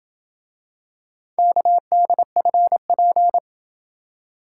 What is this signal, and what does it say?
Morse code, heard as a keyed tone, about 26 wpm, 700 Hz. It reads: KDFP